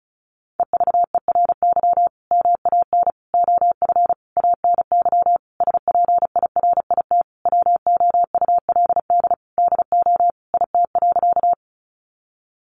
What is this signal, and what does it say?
Morse code, heard as a keyed tone, about 34 wpm, 700 Hz. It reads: EVERY MAN OF ANY SPIRIT WOULD DO IT.
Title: Sentences from Top 1400 Words 35wpm